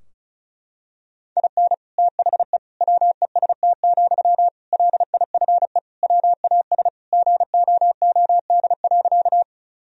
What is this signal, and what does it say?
Morse code, heard as a keyed tone, about 34 wpm, 700 Hz. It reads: IN THE WEST, LIFE WAS GOOD.
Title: Sentences from Top 1000 Words - Review Part 2 - 35wpm